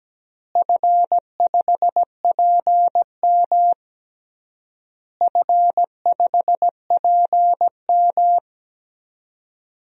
Morse code, heard as a keyed tone, about 17 wpm, 700 Hz. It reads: F5PM F5PM
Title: DXCC-100 Call Signs - 4 Characters - Repeated 3x - 17wpm